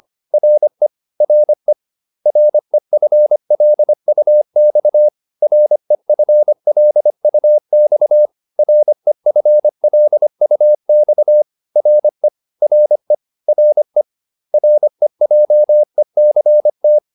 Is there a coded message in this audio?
Yes: Morse code from 0.3 to 17.0 s, heard as a keyed tone, about 25 wpm, 600 Hz. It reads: RE RE REFLUX REFLUX REFLUX RE RE RE REJECT